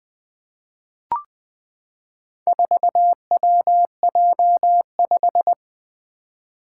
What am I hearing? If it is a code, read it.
Morse code, heard as a keyed tone, about 20 wpm, 700 Hz. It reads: 4WJ5
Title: Four Challenging Letter-Number Combinations 20wpm